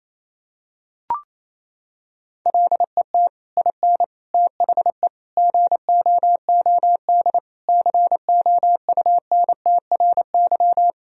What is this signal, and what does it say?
Morse code, heard as a keyed tone, about 28 wpm, 700 Hz. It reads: LET IN THE GOOD COUNTRY
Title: Sentences from Top 1000 Words - Review Part 1 - 28wpm